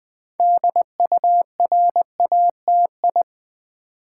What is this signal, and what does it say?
Morse code, heard as a keyed tone, about 20 wpm, 690 Hz. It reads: DURATI
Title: Build A Word - 3 Letters At A Time - 8-Letter Words - 20wpm